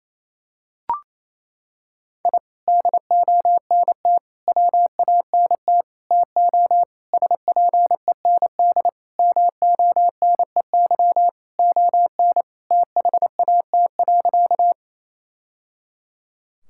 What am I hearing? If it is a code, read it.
Morse code, heard as a keyed tone, about 28 wpm, 700 Hz. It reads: I DONT WANT TO SPEND MONEY ON THAT.
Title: Sentences from Top 1000 Words - Review Part 5 - 28wpm